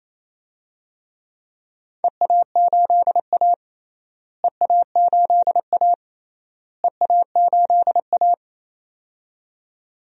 Morse code, heard as a keyed tone, about 28 wpm, 700 Hz. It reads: EA8A EA8A EA8A
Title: DXCC-100 Call Signs - 4 Characters - Repeated 3x - 28wpm